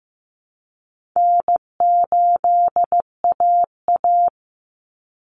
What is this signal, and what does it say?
Morse code, heard as a keyed tone, about 15 wpm, 700 Hz. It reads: N8AA
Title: Call Signs Repeated 3x - 15wpm